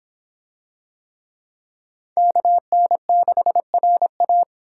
Morse code, heard as a keyed tone, about 23 wpm, 700 Hz. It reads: KN6RA